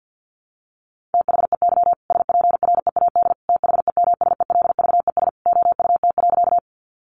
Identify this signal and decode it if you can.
Morse code, heard as a keyed tone, about 50 wpm, 700 Hz. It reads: THEY SPREAD THEMSELVES OUT.
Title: Sentences from Top 1200 Words 50wpm